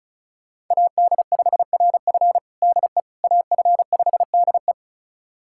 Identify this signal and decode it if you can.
Morse code, heard as a keyed tone, about 34 wpm, 700 Hz. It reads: AD5RF DE AF5DE
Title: USA Callsign DE Callsign - 5 Characters - 35wpm